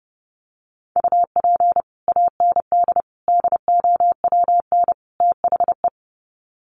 Morse code, heard as a keyed tone, about 30 wpm, 700 Hz. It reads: UP AND DOWN THE
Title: Sets of 4 Words 30wpm